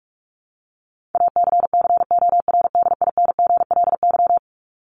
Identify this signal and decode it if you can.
Morse code, heard as a keyed tone, about 44 wpm, 700 Hz. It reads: ACCORDINGLY
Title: LY Suffix - Top 100 Words with Base Words 45wpm